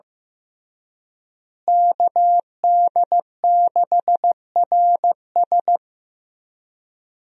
Morse code, heard as a keyed tone, about 15 wpm, 700 Hz. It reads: KD6RS
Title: USA Call Signs - 5 Characters - Repeated 3x - 15wpm